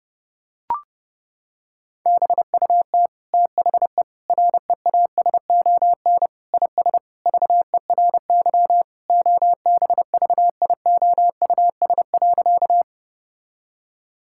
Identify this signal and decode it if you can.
Morse code, heard as a keyed tone, about 30 wpm, 700 Hz. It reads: BUT THE REASON IS VERY OBVIOUS.
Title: Sentences from Top 1600 Words 30wpm